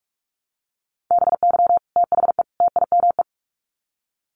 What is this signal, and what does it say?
Morse code, heard as a keyed tone, about 44 wpm, 700 Hz. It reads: BY THE TIME
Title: Sets of 3 Words 45wpm